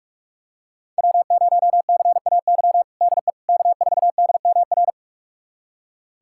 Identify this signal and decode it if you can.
Morse code, heard as a keyed tone, about 46 wpm, 700 Hz. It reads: W0CAY DE K4DMR
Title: USA Callsign DE Callsign - 5 Characters - 45wpm